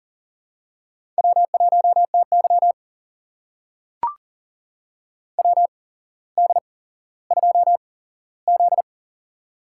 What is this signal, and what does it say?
Morse code, heard as a keyed tone, about 38 wpm, spaced as for 17 wpm, 700 Hz. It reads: W1TY WD2Z
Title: USA Call Signs - 4 Characters - 8x Character Spacing - 40wpm